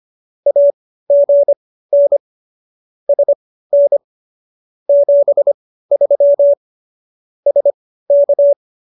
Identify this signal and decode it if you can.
Morse code, heard as a keyed tone, about 25 wpm, spaced as for 15 wpm, 570 Hz. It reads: AGN SN 73 SK